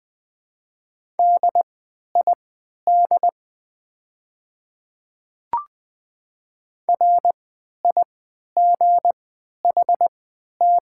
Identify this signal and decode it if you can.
Morse code, heard as a keyed tone, about 20 wpm, spaced as for 11 wpm, 690 Hz. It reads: DID RIGHT
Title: 3x Character Spacing - Top 300 Words - 20wpm